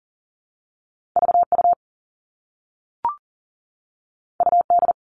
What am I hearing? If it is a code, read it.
Morse code, heard as a keyed tone, about 38 wpm, 700 Hz. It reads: VU UD
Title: All Cut Numbers - 2-Digit 40wpm